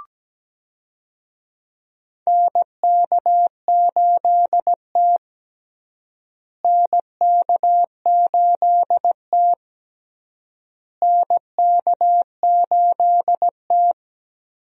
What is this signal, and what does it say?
Morse code, heard as a keyed tone, about 17 wpm, 700 Hz. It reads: NK8T NK8T NK8T
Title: USA Call Signs - 4 Characters - Repeated 3x - 17wpm